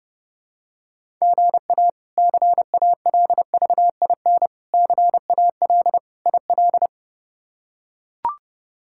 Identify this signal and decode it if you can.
Morse code, heard as a keyed tone, about 32 wpm, 700 Hz. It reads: GA CALVIN CAL IL
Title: K1USN SST QSO Practice - 30wpm